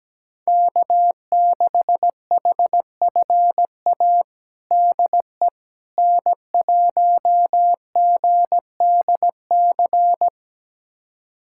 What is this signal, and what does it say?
Morse code, heard as a keyed tone, about 17 wpm, 700 Hz. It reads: K6HFA DE N1GDC
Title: USA Callsign DE Callsign - 5 Characters - 17wpm